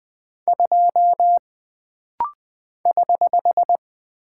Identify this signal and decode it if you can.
Morse code, heard as a keyed tone, about 20 wpm, 700 Hz. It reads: 2 <HH>
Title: Single Character (Letter:Number:Punctuation:Prosign) - Rapid-Fire - 20wpm